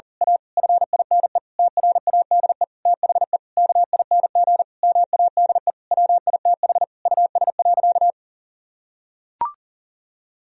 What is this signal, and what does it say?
Morse code, heard as a keyed tone, about 40 wpm, 690 Hz. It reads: A FINE TRADE THE KING MADE WITH US.
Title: Sentences from Top 1000 Words - Review Part 4 - 40wpm